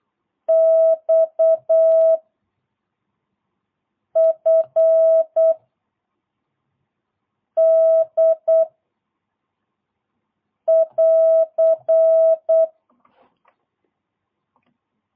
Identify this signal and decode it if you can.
Morse code, heard as a keyed tone, about 8 wpm, 660 Hz. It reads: X F D <AR>
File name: jh8cbh / 2022cw